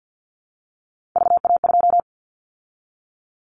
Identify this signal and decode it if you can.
Morse code, heard as a keyed tone, about 50 wpm, 700 Hz. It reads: VA2
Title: Callsign Prefixes for Top 100 DXCC - 50wpm